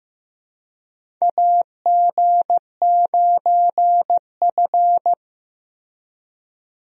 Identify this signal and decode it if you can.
Morse code, heard as a keyed tone, about 15 wpm, 700 Hz. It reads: AG9F
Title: USA Call Signs - 4 Characters - Repeated 3x - 15wpm